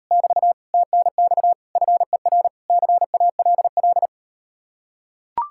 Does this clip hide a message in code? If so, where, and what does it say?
Yes: Morse code from 0.1 to 4.1 s, heard as a keyed tone, about 38 wpm, 700 Hz. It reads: <BT> TNX FER CALL